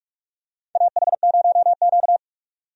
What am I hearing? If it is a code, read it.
Morse code, heard as a keyed tone, about 46 wpm, 700 Hz. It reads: AH0Q
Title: USA Call Signs - 4 Characters - 45wpm